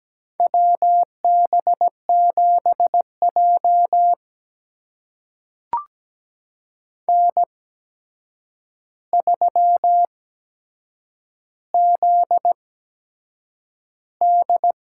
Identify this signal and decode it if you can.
Morse code, heard as a keyed tone, about 17 wpm, spaced as for 7 wpm, 700 Hz. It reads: WB7J N3ZD
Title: USA Call Signs - 4 Characters - 8x Character Spacing - 17wpm